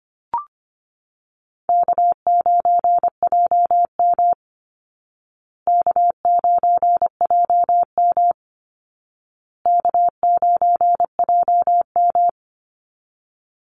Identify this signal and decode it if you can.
Morse code, heard as a keyed tone, about 25 wpm, 700 Hz. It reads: K9JM K9JM K9JM
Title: Call Signs Repeated 2x - 25wpm